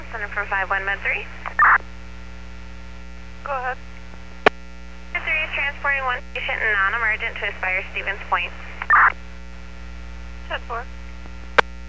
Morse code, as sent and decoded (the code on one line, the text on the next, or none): none